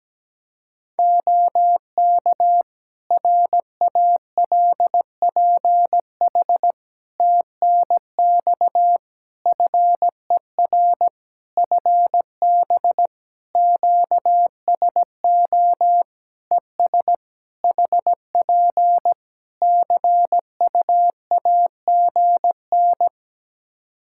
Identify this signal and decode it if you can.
Morse code, heard as a keyed tone, about 17 wpm, 700 Hz. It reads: OK RALPH TNX FER FB QSO ES HP CUAGN
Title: Practice QSO - Exchange 4 - Ending - 17wpm